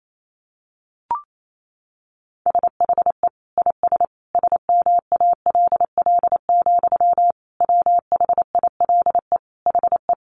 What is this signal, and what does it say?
Morse code, heard as a keyed tone, about 28 wpm, 700 Hz. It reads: SHE IS SMALL, WHILE HE
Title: Sentences from Top 1300 Words 28wpm